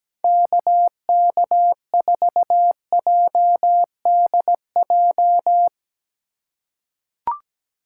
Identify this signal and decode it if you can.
Morse code, heard as a keyed tone, about 17 wpm, 700 Hz. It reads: KK4JDJ
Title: USA Call Signs - 3x Character Spacing - 17wpm